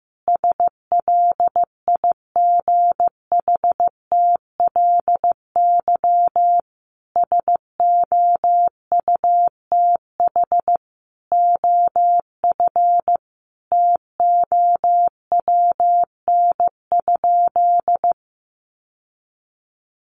Morse code, heard as a keyed tone, about 15 wpm, 700 Hz. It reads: SLIGHTLY SOUTH OF TOWN?
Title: Sentences from Top 1300 Words 15wpm